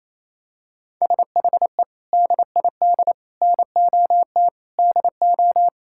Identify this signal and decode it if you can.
Morse code, heard as a keyed tone, about 28 wpm, 700 Hz. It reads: SHE DID NOT DO
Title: Sentences from Top 1000 Words - Review Part 5 - 28wpm